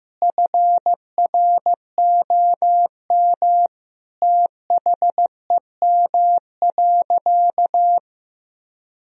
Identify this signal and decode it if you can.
Morse code, heard as a keyed tone, about 15 wpm, 700 Hz. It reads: FROM THEM.
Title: Sentences from Top 1000 Words - Review Part 2 - 15wpm